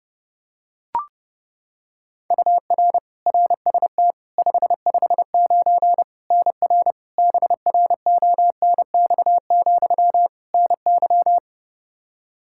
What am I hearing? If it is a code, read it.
Morse code, heard as a keyed tone, about 30 wpm, 700 Hz. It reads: UR RST 559 NR BRONX, NY
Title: K1USN SST QSO Practice 30wpm